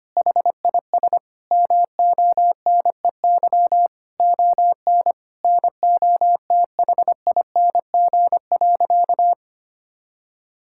Morse code, heard as a keyed tone, about 25 wpm, 700 Hz. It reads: HIS MONEY ON NOTHING.
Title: Sentences from Top 1000 Words - Review Part 5 - 25wpm